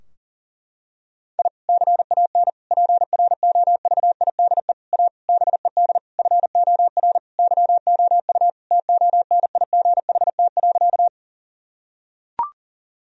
Morse code, heard as a keyed tone, about 37 wpm, 700 Hz. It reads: I CAN PROVIDE A BED FOR YOU TONIGHT.
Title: Sentences from Top 1000 Words - Review Part 3 - 40wpm